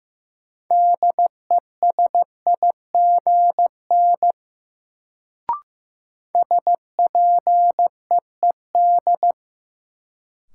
Morse code, heard as a keyed tone, about 15 wpm, 700 Hz. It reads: DESIGN SPEED
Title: Sentences from Top 800 Words - New Word Drill - 15wpm